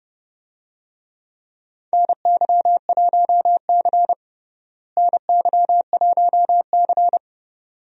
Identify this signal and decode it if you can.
Morse code, heard as a keyed tone, about 30 wpm, 700 Hz. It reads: NY1C NY1C
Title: USA Call Signs - 4 Characters - Repeated 2x - 30wpm